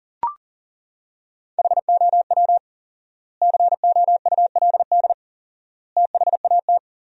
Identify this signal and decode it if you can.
Morse code, heard as a keyed tone, about 38 wpm, 700 Hz. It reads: HOW COULD THAT
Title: 4x Word-Spacing - Sentences from Top 400 Words - 40wpm